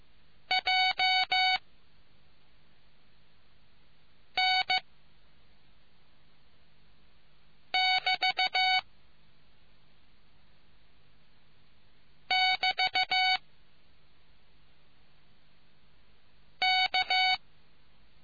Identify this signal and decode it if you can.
Morse code, heard as a keyed tone, about 15 wpm, spaced as for 3 wpm, 750 Hz. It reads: JN<BT><BT>K